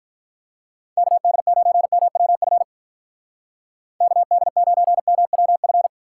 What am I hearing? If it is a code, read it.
Morse code, heard as a keyed tone, about 43 wpm, 700 Hz. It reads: KD9MWF KD9MWF
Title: Speed Racing - USA 2x3 Call Signs 35wpm